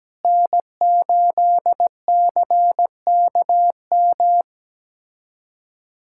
Morse code, heard as a keyed tone, about 17 wpm, 700 Hz. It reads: N8CKM
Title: USA Callsign DE Callsign - 5 Characters - 17wpm